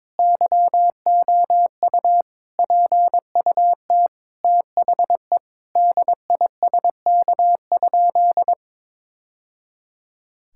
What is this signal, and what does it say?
Morse code, heard as a keyed tone, about 22 wpm, 700 Hz. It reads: YOU PUT THE DISK?
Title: Sentences from Top 1000 Words - Review Part 6 - 22wpm